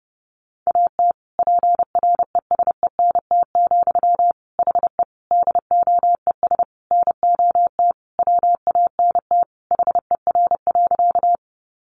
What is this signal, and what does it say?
Morse code, heard as a keyed tone, about 30 wpm, 700 Hz. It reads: AT PRESENT, HE DOES NOT WANT HER.
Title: Sentences from Top 800 Words 30wpm